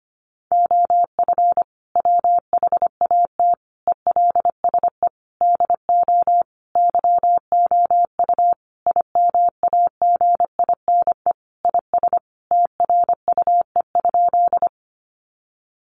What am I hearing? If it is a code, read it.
Morse code, heard as a keyed tone, about 25 wpm, 700 Hz. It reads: OF WHAT ELSE DO YOU IMAGINE IS TRUE?
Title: Sentences from Top 1400 Words 25wpm